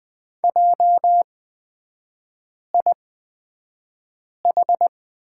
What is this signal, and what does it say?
Morse code, heard as a keyed tone, about 20 wpm, spaced as for 5 wpm, 700 Hz. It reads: JIH
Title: 229 Lesson 37e - Letter x - Three Characters 20wpm